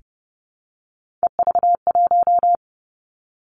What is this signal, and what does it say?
Morse code, heard as a keyed tone, about 30 wpm, 700 Hz. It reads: EV1
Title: Three Challenging Letter-Number Combinations 30wpm